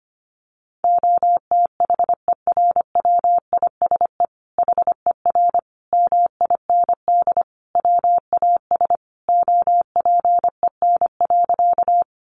Morse code, heard as a keyed tone, about 25 wpm, 700 Hz. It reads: OTHERWISE HER MIND WAS OPEN.
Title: Sentences from Top 1200 Words 25wpm